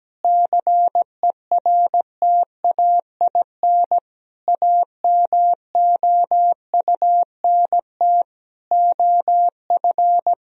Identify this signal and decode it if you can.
Morse code, heard as a keyed tone, about 17 wpm, 700 Hz. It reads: CERTAIN AMOUNT OF
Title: Sentences from Top 1000 Words - Review Part 4 - 17wpm